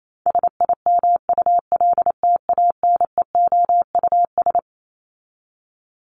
Morse code, heard as a keyed tone, about 28 wpm, 700 Hz. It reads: SIMULTANEOUS
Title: LY Suffix - Top 100 Words with Base Words 28wpm